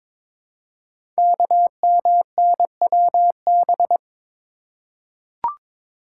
Morse code, heard as a keyed tone, about 22 wpm, 700 Hz. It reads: KMNWB
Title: Code Groups - 5 Letters - 22wpm